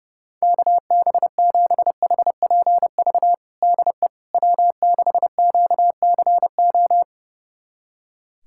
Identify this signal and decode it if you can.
Morse code, heard as a keyed tone, about 32 wpm, 700 Hz. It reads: KB7HPV DE W6QCO
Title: USA Callsign DE Callsign 30wpm